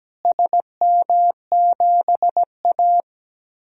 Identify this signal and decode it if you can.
Morse code, heard as a keyed tone, about 17 wpm, 700 Hz. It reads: SM7A
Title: DXCC-100 Call Signs - 4 Characters - Repeated 3x - 17wpm